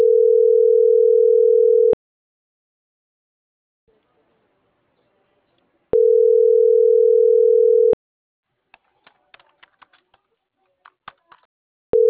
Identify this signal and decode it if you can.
Ringback tone (North America), 440 + 480 Hz.